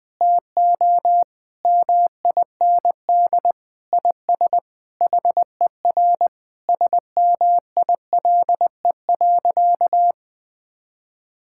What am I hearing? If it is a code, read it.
Morse code, heard as a keyed tone, about 20 wpm, 700 Hz. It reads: TO MIND IS HER SMILE.
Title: Sentences with 8-Word Phrases - 20wpm